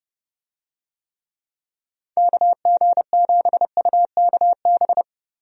Morse code, heard as a keyed tone, about 30 wpm, 700 Hz. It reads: KG7UKB
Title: USA Call Signs - 6 Characters - Repeated 3x - 30wpm